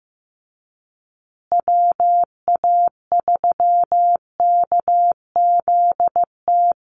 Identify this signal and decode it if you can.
Morse code, heard as a keyed tone, about 15 wpm, 700 Hz. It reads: WA3KZT
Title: USA Call Signs Repeated 3x 15wpm